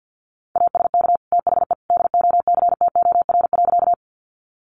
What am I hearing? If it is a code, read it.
Morse code, heard as a keyed tone, about 50 wpm, 700 Hz. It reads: ASK THE DOCTOR.
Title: Sentences from Top 1500 Words 50wpm